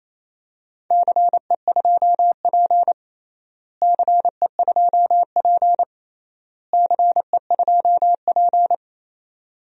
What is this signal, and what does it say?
Morse code, heard as a keyed tone, about 28 wpm, 700 Hz. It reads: CE2P CE2P CE2P
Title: DXCC-100 Call Signs - 4 Characters - Repeated 3x - 28wpm